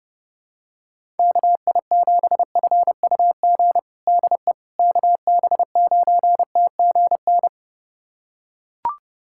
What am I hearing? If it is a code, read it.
Morse code, heard as a keyed tone, about 30 wpm, 700 Hz. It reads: KI7FUG DE KB9TGN
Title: USA Callsign DE Callsign - 6 Characters - 30wpm